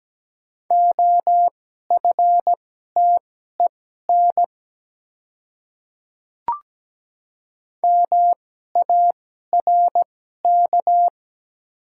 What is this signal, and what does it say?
Morse code, heard as a keyed tone, about 17 wpm, spaced as for 12 wpm, 700 Hz. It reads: OFTEN MARK
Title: 2x Character Spacing - Top 300 Words - 17wpm